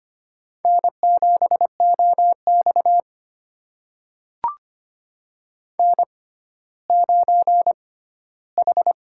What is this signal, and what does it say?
Morse code, heard as a keyed tone, about 25 wpm, spaced as for 14 wpm, 700 Hz. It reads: N7OX N9H
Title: USA Call Signs - 6x Character Spacing - 25wpm